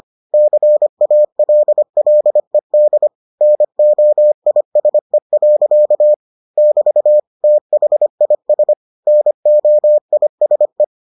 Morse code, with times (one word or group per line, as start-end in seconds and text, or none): 0.3-3.1 CALLED
3.4-6.1 NOISE.
6.6-7.2 <BT>
7.4-8.7 THIS
9.1-10.8 NOISE